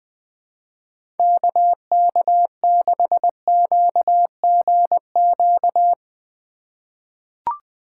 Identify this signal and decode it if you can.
Morse code, heard as a keyed tone, about 20 wpm, 700 Hz. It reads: KK6QGQ
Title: USA Call Signs - 6 Characters - 4x Character Spacing - 20wpm